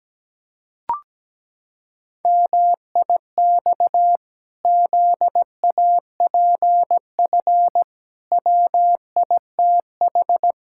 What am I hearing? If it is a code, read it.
Morse code, heard as a keyed tone, about 17 wpm, 700 Hz. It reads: MIX ZAPF WITH
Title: Pangram Challenge v2 - 17wpm